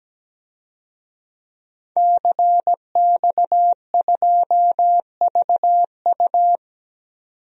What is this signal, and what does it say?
Morse code, heard as a keyed tone, about 17 wpm, 700 Hz. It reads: CX2VU
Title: DXCC-100 Call Signs - 5 Characters - 17wpm